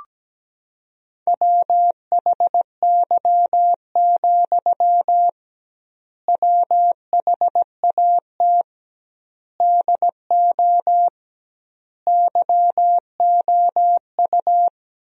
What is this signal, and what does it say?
Morse code, heard as a keyed tone, about 17 wpm, 700 Hz. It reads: WHY, WHAT DO YOU
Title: 2x Word-Spacing - Sentences from Top 200 Words - 17wpm